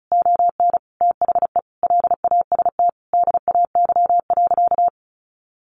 Morse code, heard as a keyed tone, about 34 wpm, 700 Hz. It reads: ON THE LAST DAY.
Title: Sentences from Top 1400 Words 35wpm